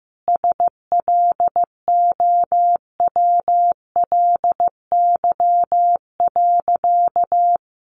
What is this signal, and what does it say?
Morse code, heard as a keyed tone, about 14 wpm, 700 Hz. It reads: SLOWLY.
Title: Additional Easy Phrases 15wpm